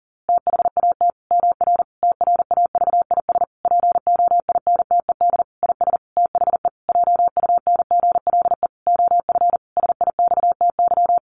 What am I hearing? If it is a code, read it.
Morse code, heard as a keyed tone, about 40 wpm, 700 Hz. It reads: THAT MR TRAVIS POINTED IS THE JUNGLE OF SIXTY
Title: A Sound Of Thunder 40wpm